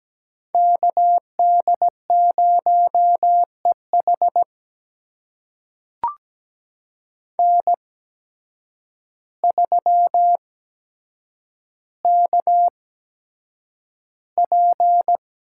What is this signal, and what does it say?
Morse code, heard as a keyed tone, about 17 wpm, spaced as for 8 wpm, 700 Hz. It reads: KD0EH N3KP
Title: USA Call Signs - 5 Characters - 8x Character Spacing - 17wpm